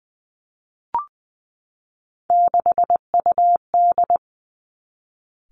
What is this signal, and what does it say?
Morse code, heard as a keyed tone, about 20 wpm, 700 Hz. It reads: 6UD
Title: Three Challenging Letter-Number Combinations 20wpm